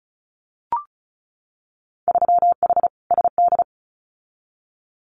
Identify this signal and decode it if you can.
Morse code, heard as a keyed tone, about 36 wpm, 700 Hz. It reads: 3H SD
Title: Winter Field Day Exchanges Encore 35wpm